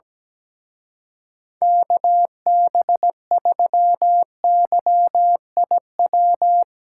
Morse code, heard as a keyed tone, about 17 wpm, 700 Hz. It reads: KB3YIW
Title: Speed Racing - USA 2x3 Call Signs 17wpm